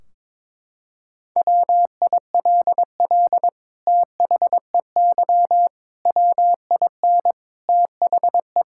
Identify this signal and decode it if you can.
Morse code, heard as a keyed tone, about 22 wpm, 700 Hz. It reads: WILL THEY WIN THE
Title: Sentences from Top 1600 Words 22wpm